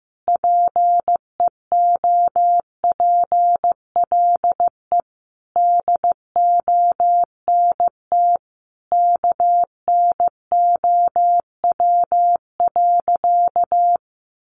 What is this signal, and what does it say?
Morse code, heard as a keyed tone, about 15 wpm, 700 Hz. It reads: PEOPLE DONT KNOW.
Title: Sentences from Top 600 Words 15wpm